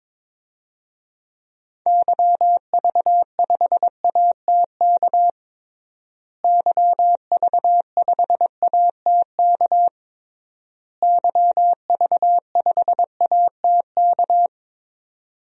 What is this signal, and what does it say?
Morse code, heard as a keyed tone, about 22 wpm, 700 Hz. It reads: YV5ATK YV5ATK YV5ATK
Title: DXCC-100 Call Signs - 6 Characters - Repeated 3x - 22wpm